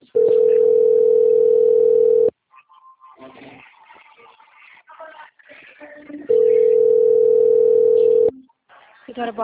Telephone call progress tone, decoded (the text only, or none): ringback tone